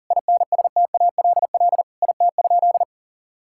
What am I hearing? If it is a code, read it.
Morse code, heard as a keyed tone, about 40 wpm, 700 Hz. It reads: INSTALL IT?
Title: Sentences from Top 1600 Words 40wpm